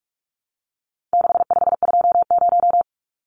Morse code, heard as a keyed tone, about 44 wpm, 700 Hz. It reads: 6520
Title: Numbers - 4-Digit 45wpm